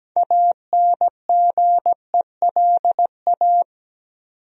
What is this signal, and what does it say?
Morse code, heard as a keyed tone, about 17 wpm, 700 Hz. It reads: ANGELA
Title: USA Common Names - 17wpm